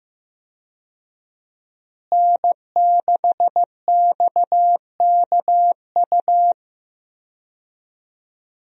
Morse code, heard as a keyed tone, about 15 wpm, 700 Hz. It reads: N6XKU
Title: USA Call Signs - 5 Characters - Repeated 3x - 15wpm